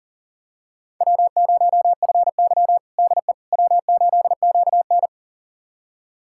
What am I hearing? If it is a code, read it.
Morse code, heard as a keyed tone, about 40 wpm, 700 Hz. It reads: W0FY DE W8QN
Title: USA Callsign DE Callsign - 4 Characters - 40wpm